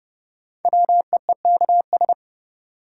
Morse code, heard as a keyed tone, about 30 wpm, 700 Hz. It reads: WEEKS
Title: Sets of 5 Words - New Word Drill - 30wpm